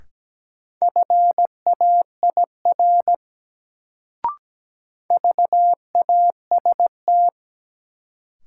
Morse code, heard as a keyed tone, about 17 wpm, 700 Hz. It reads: FAIR VAST
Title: Sentences from Top 1300 Words - New Word Drill Alt - 17wpm